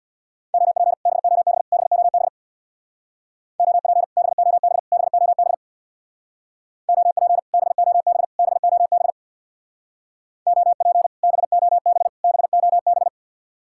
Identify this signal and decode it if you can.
Morse code, heard as a keyed tone, about 62 wpm, 700 Hz. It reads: OP BOB BOB OP BOB BOB OP BOB BOB OP BOB BOB
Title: Speed-Racing Common QSO Phrases - 50wpm